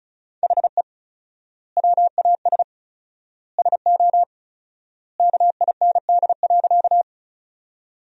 Morse code, heard as a keyed tone, about 36 wpm, 700 Hz. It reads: HE WAS SO KIND.
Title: 4x Word-Spacing - Sentences from Top 200 Words - 35wpm